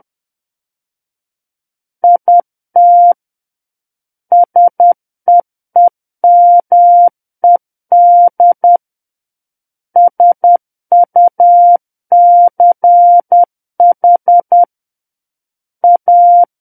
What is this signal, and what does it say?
Morse code, heard as a keyed tone, about 10 wpm, 700 Hz. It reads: IT SEEMED SUCH A
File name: Book One / 01-Chapter 1 - 10 WPM